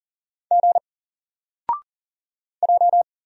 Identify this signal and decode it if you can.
Morse code, heard as a keyed tone, about 40 wpm, 700 Hz. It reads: G J
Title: Single Letter-Number 40wpm